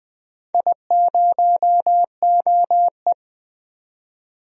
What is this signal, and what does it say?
Morse code, heard as a keyed tone, about 20 wpm, 700 Hz. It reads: I0OE
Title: DXCC-100 Call Signs - 4 Characters - 20wpm